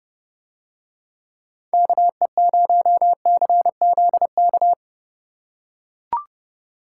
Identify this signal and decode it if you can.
Morse code, heard as a keyed tone, about 32 wpm, 700 Hz. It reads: KE0CZK